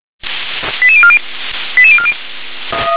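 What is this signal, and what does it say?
Five-tone selective call (ZVEI), heard as quick runs of five tones, tones of about 70 ms. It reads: 88040 88040
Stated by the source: ZVEI five-tone selective call received on SDR.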